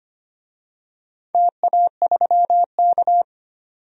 Morse code, heard as a keyed tone, about 25 wpm, 700 Hz. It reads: TA3K
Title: DXCC-100 Call Signs - 4 Characters - 25wpm